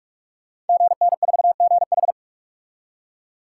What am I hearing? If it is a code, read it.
Morse code, heard as a keyed tone, about 47 wpm, 700 Hz. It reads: GN4GH
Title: Code Groups - 5 Letter:Numbers - 45wpm